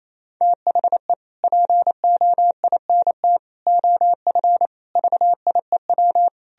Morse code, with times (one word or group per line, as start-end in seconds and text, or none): 0.4-1.1 THE
1.4-3.4 POINT
3.7-4.7 OF
5.0-6.3 VIEW